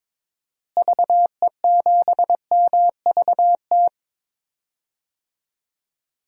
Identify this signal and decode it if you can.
Morse code, heard as a keyed tone, about 22 wpm, 700 Hz. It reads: VE7MVT
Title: DXCC-100 Call Signs - 6 Characters - Repeated 2x - 22wpm